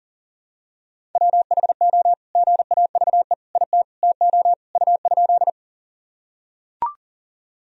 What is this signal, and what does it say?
Morse code, heard as a keyed tone, about 40 wpm, 700 Hz. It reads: WHO GAVE IT TO U?
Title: Sentences from Top 500 Words and Abbreviations 40wpm